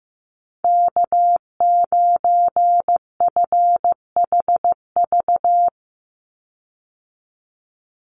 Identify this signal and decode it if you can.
Morse code, heard as a keyed tone, about 15 wpm, 700 Hz. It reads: K9FHV
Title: USA Call Signs 15wpm